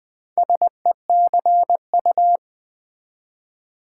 Morse code, heard as a keyed tone, about 20 wpm, 700 Hz. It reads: SECU
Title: Build A Word - 2 Letters At A Time - 6-Letter Words - 20wpm